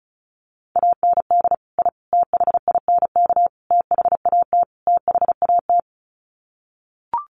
Morse code, heard as a keyed tone, about 34 wpm, 700 Hz. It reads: AND I THINK THAT THAT
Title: Sets of 5 Words Encore 35wpm